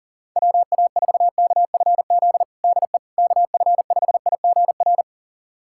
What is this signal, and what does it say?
Morse code, heard as a keyed tone, about 37 wpm, 700 Hz. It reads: WA4KFZ DE KF5IGR
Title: USA Callsign DE Callsign - 6 Characters - 40wpm